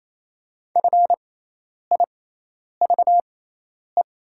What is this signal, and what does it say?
Morse code, heard as a keyed tone, about 28 wpm, spaced as for 10 wpm, 700 Hz. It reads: FIVE
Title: 6x Character Spacing - Top 500 Words - 28wpm